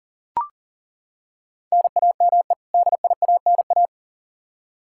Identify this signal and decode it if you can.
Morse code, heard as a keyed tone, about 37 wpm, 700 Hz. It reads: NAME DIANA
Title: Practice QSO - Exchange 1 - Beginning - 40wpm